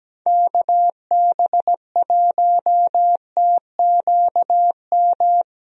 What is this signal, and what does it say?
Morse code, heard as a keyed tone, about 17 wpm, 700 Hz. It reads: KB1TQM
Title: USA Call Signs - 6 Characters - Repeated 3x - 17wpm